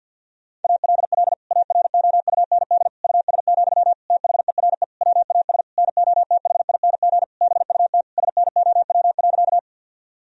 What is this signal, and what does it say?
Morse code, heard as a keyed tone, about 50 wpm, 700 Hz. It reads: ALL AROUND US, THERE WAS NOTHING BUT SNOW.
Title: Sentences from Top 1000 Words - Review Part 3 - 50wpm